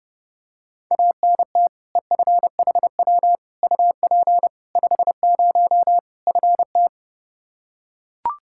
Morse code, heard as a keyed tone, about 32 wpm, 700 Hz. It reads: ANT EFHW UP 50 FT
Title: Practice QSO - Exchange 2 - Info - 30wpm